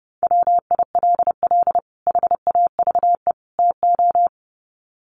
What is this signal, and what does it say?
Morse code, heard as a keyed tone, about 30 wpm, 700 Hz. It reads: WILL HAVE TO
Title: Sets of 3 Words 30wpm